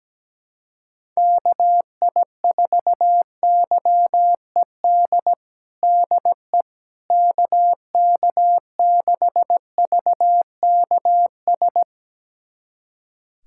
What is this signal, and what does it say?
Morse code, heard as a keyed tone, about 17 wpm, 700 Hz. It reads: KI4YED DE KK6VKS
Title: USA Callsign DE Callsign - 6 Characters - 17wpm